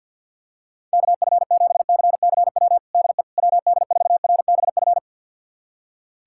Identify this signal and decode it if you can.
Morse code, heard as a keyed tone, about 52 wpm, 700 Hz. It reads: KF7CCW DE WD4RBF
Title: USA Callsign DE Callsign - 6 Characters - 50wpm